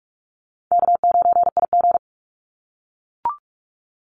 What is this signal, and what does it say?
Morse code, heard as a keyed tone, about 44 wpm, 700 Hz. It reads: K9IG
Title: Easy Call Signs - 45wpm